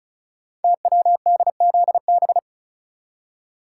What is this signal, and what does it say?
Morse code, heard as a keyed tone, about 36 wpm, 690 Hz. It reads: TWDZB
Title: Code Groups - 5 Letters - 35wpm